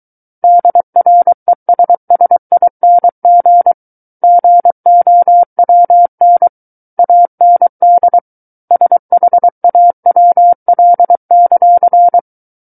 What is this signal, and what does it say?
Morse code, heard as a keyed tone, about 23 wpm, 700 Hz. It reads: DRESSING GOWN AND SHAWL;
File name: Book One / 14-Chapter 14 - 23 WPM